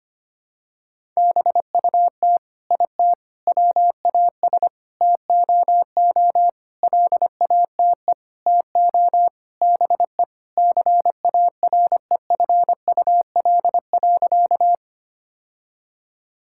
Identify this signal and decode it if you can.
Morse code, heard as a keyed tone, about 25 wpm, 700 Hz. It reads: BUT IT WAS TOO LATE TO BE CAREFUL.
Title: Sentences from Top 1600 Words 25wpm